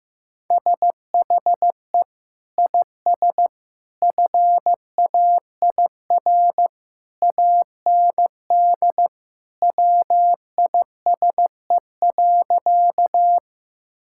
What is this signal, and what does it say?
Morse code, heard as a keyed tone, about 15 wpm, 700 Hz. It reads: SHE IS FAIR AND WISE.
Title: Sentences from Top 1600 Words 15wpm